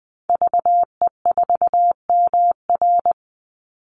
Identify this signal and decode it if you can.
Morse code, heard as a keyed tone, about 20 wpm, 700 Hz. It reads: VE4MR
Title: Call Signs Repeated 3x - 20wpm